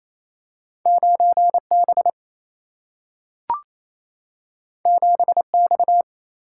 Morse code, heard as a keyed tone, about 28 wpm, 700 Hz. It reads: 9B 7X
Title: Single-Letter Single-Number 28wpm